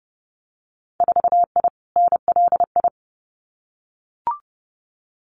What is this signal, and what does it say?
Morse code, heard as a keyed tone, about 30 wpm, 700 Hz. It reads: 4I NLI
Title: Winter Field Day Exchanges Encore 30wpm